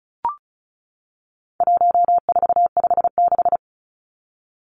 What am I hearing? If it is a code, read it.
Morse code, heard as a keyed tone, about 34 wpm, 700 Hz. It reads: 1456
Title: Numbers - 4-Digit 35wpm